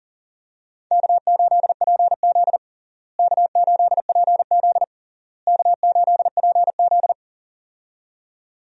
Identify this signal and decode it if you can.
Morse code, heard as a keyed tone, about 40 wpm, 700 Hz. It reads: K8PZ K8PZ K8PZ
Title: USA Call Signs - 4 Characters - Repeated 3x - 40wpm